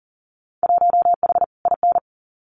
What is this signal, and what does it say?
Morse code, heard as a keyed tone, about 38 wpm, 700 Hz. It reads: 1H IN
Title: Winter Field Day Exchanges 40wpm